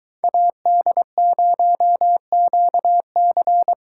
Morse code, heard as a keyed tone, about 20 wpm, 700 Hz. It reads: AD0QC